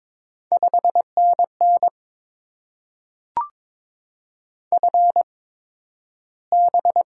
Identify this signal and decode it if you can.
Morse code, heard as a keyed tone, about 22 wpm, spaced as for 12 wpm, 700 Hz. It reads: 5NN FB
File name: Top 20 QSO Elements - 8x Character Spacing - 22wpm